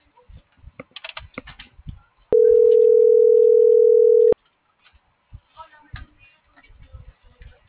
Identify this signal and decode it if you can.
Ringback tone (North America), 440 + 480 Hz.